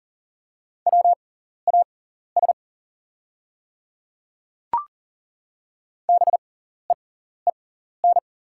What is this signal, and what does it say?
Morse code, heard as a keyed tone, about 40 wpm, spaced as for 14 wpm, 690 Hz. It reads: WAS BEEN
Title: 6x Character Spacing - Top 300 Words - 40wpm